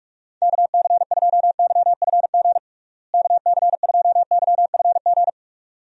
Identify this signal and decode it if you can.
Morse code, heard as a keyed tone, about 44 wpm, 700 Hz. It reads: KC2YFG KC2YFG
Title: USA Call Signs - 6 Characters - Repeated 2x - 45wpm